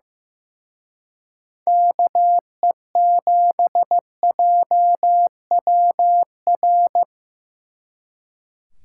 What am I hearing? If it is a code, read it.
Morse code, heard as a keyed tone, about 18 wpm, 700 Hz. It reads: KE7JWR
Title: Speed Racing - USA Call Signs 15wpm